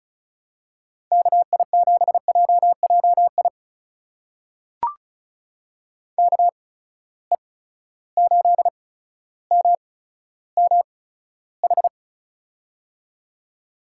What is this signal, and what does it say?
Morse code, heard as a keyed tone, about 36 wpm, spaced as for 15 wpm, 700 Hz. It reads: KI7JJI KE8MMH
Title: USA Call Signs - 6 Characters - 8x Character Spacing - 35wpm